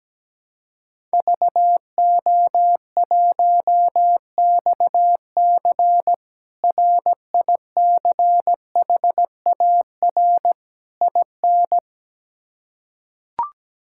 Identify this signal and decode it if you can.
Morse code, heard as a keyed tone, about 17 wpm, 700 Hz. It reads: VO1XC RICHAR IN
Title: K1USN SST QSO Practice - 17wpm